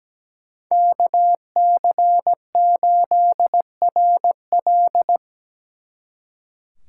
Morse code, heard as a keyed tone, about 20 wpm, 700 Hz. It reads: KC8RL
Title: Speed Racing - USA 2x2 Call Signs 17wpm